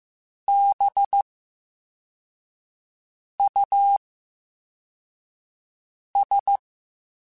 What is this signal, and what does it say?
Morse code, heard as a keyed tone, about 15 wpm, 800 Hz. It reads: B U S
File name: Special INGs / Special INGs-mp3s / Slow ing @13